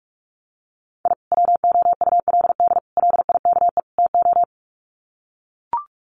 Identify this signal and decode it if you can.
Morse code, heard as a keyed tone, about 44 wpm, 700 Hz. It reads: I WOULD LIKE TO
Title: Sets of 4 Words 45wpm